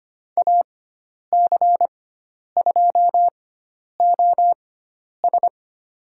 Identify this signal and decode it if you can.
Morse code, heard as a keyed tone, about 25 wpm, spaced as for 10 wpm, 700 Hz. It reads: AC2OS
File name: Lesson 16g - Letter c - Call Signs 25wpm